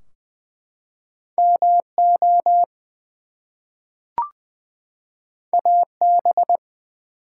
Morse code, heard as a keyed tone, about 20 wpm, 700 Hz. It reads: MO AB
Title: US States and Canadian Provinces 20wpm